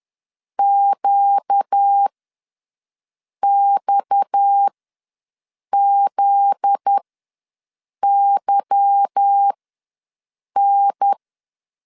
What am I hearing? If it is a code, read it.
Morse code, heard as a keyed tone, about 10 wpm, spaced as for 6 wpm, 800 Hz. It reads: QXZYN
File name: LEZIONI / 9 2867 ws 10wpm 4215